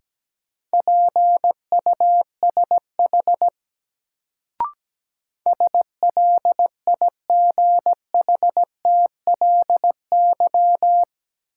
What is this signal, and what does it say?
Morse code, heard as a keyed tone, about 17 wpm, 700 Hz. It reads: PUSH SLIGHTLY
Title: Sentences from Top 1300 Words - New Word Drill Alt - 17wpm